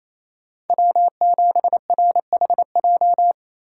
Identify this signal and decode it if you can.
Morse code, heard as a keyed tone, about 34 wpm, 700 Hz. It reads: W7RHJ
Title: Speed Racing - USA 1x3 Call Signs 28wpm